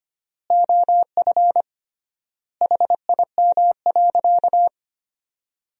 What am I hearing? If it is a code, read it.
Morse code, heard as a keyed tone, about 25 wpm, 700 Hz. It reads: OF HIM.
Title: 3x Word-Spacing - Sentences from Top 400 Words - 25wpm